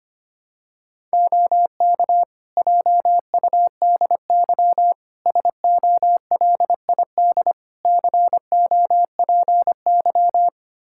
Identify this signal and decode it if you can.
Morse code, heard as a keyed tone, about 25 wpm, 700 Hz. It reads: OK JUDY SOLID COPY
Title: Practice QSO - Exchange 3 - More Info - 25wpm